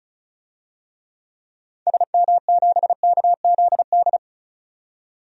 Morse code, heard as a keyed tone, about 36 wpm, 700 Hz. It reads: SM7KZD
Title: DXCC-100 Call Signs - 6 Characters - 35wpm